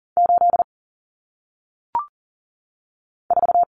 Morse code, heard as a keyed tone, about 39 wpm, 700 Hz. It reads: 8 4
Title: Numbers - 1-Digit 40wpm